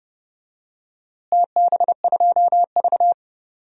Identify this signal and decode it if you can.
Morse code, heard as a keyed tone, about 30 wpm, 700 Hz. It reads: TB2V
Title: DXCC-100 Call Signs - 4 Characters - Repeated 3x - 30wpm